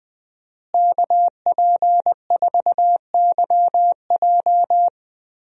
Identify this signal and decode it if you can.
Morse code, heard as a keyed tone, about 20 wpm, 700 Hz. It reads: KP4YJ
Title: DXCC-100 Call Signs - 5 Characters - Repeated 2x - 20wpm